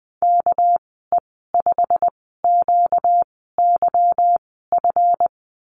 Morse code, heard as a keyed tone, about 19 wpm, spaced as for 14 wpm, 700 Hz. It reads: KE5QYF
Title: 238 Lesson 38g - Letter q - Call Signs 20wpm